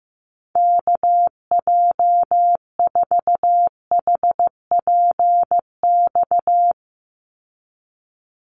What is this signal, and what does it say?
Morse code, heard as a keyed tone, about 15 wpm, 700 Hz. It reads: KJ4HPX
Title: USA Call Signs 15wpm